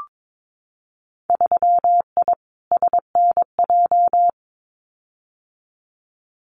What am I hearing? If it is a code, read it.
Morse code, heard as a keyed tone, about 22 wpm, 700 Hz. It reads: 3I SNJ
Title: Winter Field Day Exchanges Encore 22wpm